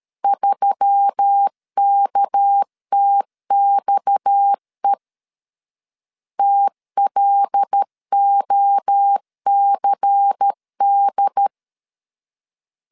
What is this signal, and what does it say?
Morse code, heard as a keyed tone, about 13 wpm, 800 Hz. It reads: 3KTXE TLOCD
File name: LEZIONI / 6 PKX0 / 6 PKX 12wpm 2024